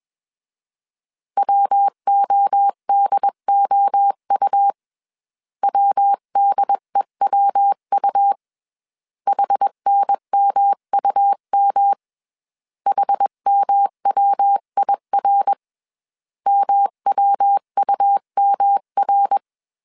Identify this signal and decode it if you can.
Morse code, heard as a keyed tone, about 21 wpm, 800 Hz. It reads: WODOU WDEWU HNMUM HMWIR MWUMR
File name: 20wpm